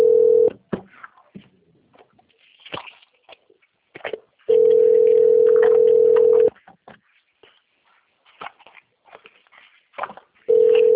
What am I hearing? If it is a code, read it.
Ringback tone (North America), 440 + 480 Hz.
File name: out- - - - - .1791